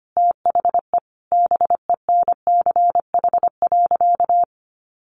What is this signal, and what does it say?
Morse code, heard as a keyed tone, about 25 wpm, 700 Hz. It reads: THE BENCH.
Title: Sentences from Top 1400 Words 25wpm